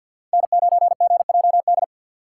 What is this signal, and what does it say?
Morse code, heard as a keyed tone, about 53 wpm, 700 Hz. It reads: N9GJD
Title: USA Call Signs - 5 Characters - 2x Character Spacing - 50wpm